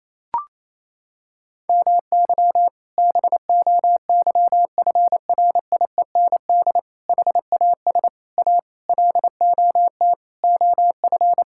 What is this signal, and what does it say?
Morse code, heard as a keyed tone, about 28 wpm, 700 Hz. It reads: MY BOYFRIEND HAS A LOT OF
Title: Sentences from Top 1000 Words - Review Part 5 - 28wpm